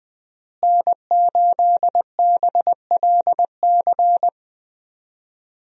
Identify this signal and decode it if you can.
Morse code, heard as a keyed tone, about 21 wpm, 700 Hz. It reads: N8BLC
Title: USA Call Signs - 5 Characters - Repeated 2x - 20wpm